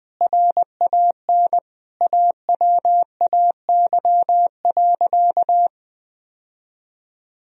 Morse code, heard as a keyed tone, about 21 wpm, 700 Hz. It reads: RAN AWAY.
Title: Sentences from Top 1000 Words - Review Part 3 - 20wpm